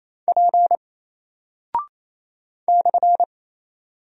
Morse code, heard as a keyed tone, about 27 wpm, 700 Hz. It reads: P /
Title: Single Character (Letter:Number:Punctuation:Prosign) Alt 28wpm